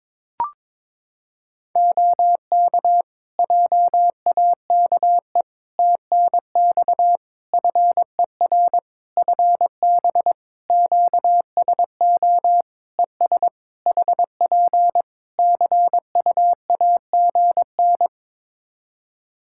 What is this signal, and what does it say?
Morse code, heard as a keyed tone, about 22 wpm, 700 Hz. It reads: OK JAKE TNX FER FB QSO ES HP CUAGN
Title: Practice QSO - Exchange 4 - Ending - 22wpm